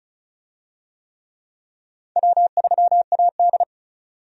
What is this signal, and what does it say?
Morse code, heard as a keyed tone, about 35 wpm, 700 Hz. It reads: W3AD
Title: USA Call Signs - 4 Characters - 35wpm